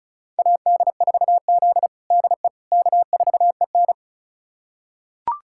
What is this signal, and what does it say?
Morse code, heard as a keyed tone, about 36 wpm, 700 Hz. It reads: AD4Z DE K4EN
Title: USA Callsign DE Callsign - 4 Characters - 35wpm